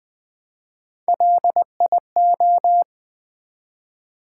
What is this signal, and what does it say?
Morse code, heard as a keyed tone, about 20 wpm, 700 Hz. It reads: LIO
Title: Build A Word - 1 Letter At A Time - 4-Letter Words - 20wpm